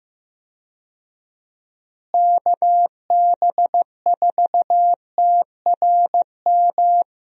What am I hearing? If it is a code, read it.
Morse code, heard as a keyed tone, about 15 wpm, 700 Hz. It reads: KB4TRM
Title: USA Call Signs - 6 Characters - Repeated 2x - 15wpm